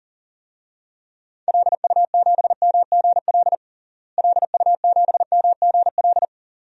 Morse code, heard as a keyed tone, about 40 wpm, 700 Hz. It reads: LU7MGL LU7MGL
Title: DXCC-100 Call Signs - 6 Characters - Repeated 2x - 40wpm